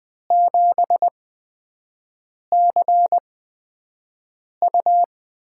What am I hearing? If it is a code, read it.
Morse code, heard as a keyed tone, about 20 wpm, spaced as for 5 wpm, 700 Hz. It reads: 7CU
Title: USA Call Signs - 5 Characters - 8x Character Spacing - 20wpm